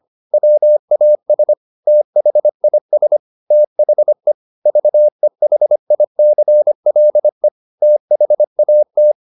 Morse code, heard as a keyed tone, about 25 wpm, 600 Hz. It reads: WAS THIS THE VEHICLE THAT